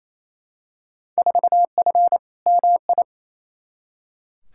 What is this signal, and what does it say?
Morse code, heard as a keyed tone, about 28 wpm, 700 Hz. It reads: 4F MI
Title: Field Day Exchanges Encore - 28wpm